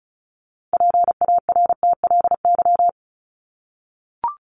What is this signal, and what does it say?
Morse code, heard as a keyed tone, about 34 wpm, 700 Hz. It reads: PARTLY
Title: LY Suffix - Top 100 Words with Base Words 35wpm